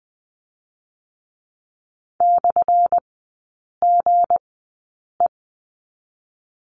Morse code, heard as a keyed tone, about 20 wpm, spaced as for 8 wpm, 700 Hz. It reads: /GE
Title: 184 Lesson 30e - Letter b - Three Characters 20wpm